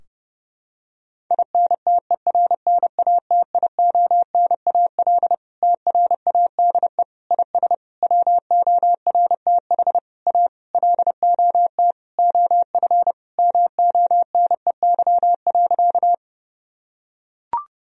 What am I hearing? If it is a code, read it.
Morse code, heard as a keyed tone, about 30 wpm, 700 Hz. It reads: INTERNATIONAL TRADE IS WORTH A LOT OF MONEY.
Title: Sentences from Top 1000 Words - Review Part 4 - 30wpm